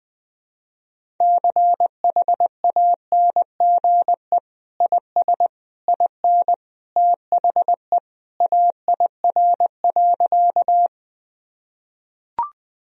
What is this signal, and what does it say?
Morse code, heard as a keyed tone, about 20 wpm, 700 Hz. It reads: CHANGE IS IN THE AIR.
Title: Sentences from Top 1000 Words - Review Part 1 - 20wpm